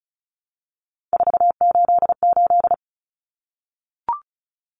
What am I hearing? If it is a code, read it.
Morse code, heard as a keyed tone, about 34 wpm, 700 Hz. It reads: 488
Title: Numbers - 3-Digit 35wpm